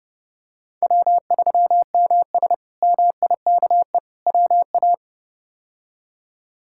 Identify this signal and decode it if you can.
Morse code, heard as a keyed tone, about 32 wpm, 700 Hz. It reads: W3MS MIKE WA
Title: K1USN SST QSO Practice - 30wpm